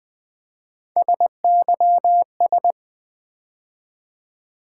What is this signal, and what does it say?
Morse code, heard as a keyed tone, about 20 wpm, 700 Hz. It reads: SYS
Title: Build A Word - 3 Letters At A Time - 6-Letter Words - 20wpm